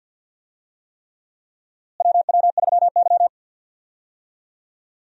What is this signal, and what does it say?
Morse code, heard as a keyed tone, about 52 wpm, 700 Hz. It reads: WW3Y
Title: USA Call Signs - 4 Characters - Repeated 3x - 50wpm